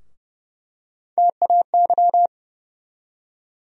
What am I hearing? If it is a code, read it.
Morse code, heard as a keyed tone, about 30 wpm, 700 Hz. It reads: TAY
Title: USA Call Sign Postfixes - 3 Letters 30wpm